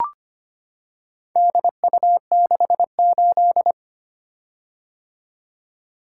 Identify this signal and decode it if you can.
Morse code, heard as a keyed tone, about 25 wpm, 700 Hz. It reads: DU68
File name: practice / 25wpm / Four Letter-Number Combinations - 25wpm